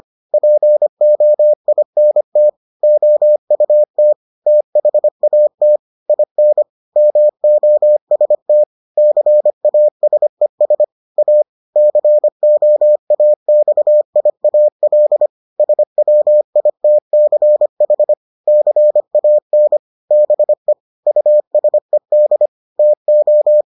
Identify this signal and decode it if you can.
Morse code, heard as a keyed tone, about 25 wpm, 600 Hz. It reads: POINT OUT THAT IN MOST CASES A COAXIAL SWITCH CAN BE USED TO